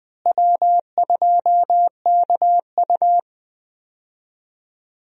Morse code, heard as a keyed tone, about 20 wpm, 700 Hz. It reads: W2KU
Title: USA Call Signs - 4 Characters - Repeated 3x - 20wpm